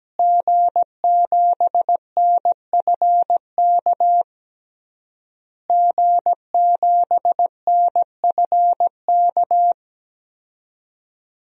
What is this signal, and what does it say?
Morse code, heard as a keyed tone, about 17 wpm, 700 Hz. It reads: G7NFK G7NFK
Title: DXCC-100 Call Signs - 5 Characters - Repeated 3x - 17wpm